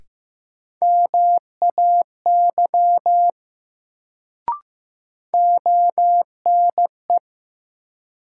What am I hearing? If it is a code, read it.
Morse code, heard as a keyed tone, about 15 wpm, 700 Hz. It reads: MAY ONE
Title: Sentences from Top 100 Words - New Word Drill - 15wpm